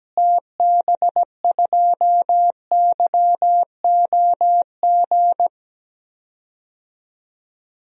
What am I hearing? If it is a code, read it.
Morse code, heard as a keyed tone, about 17 wpm, 700 Hz. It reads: TB2YOG
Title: DXCC-100 Call Signs - 6 Characters - Repeated 2x - 17wpm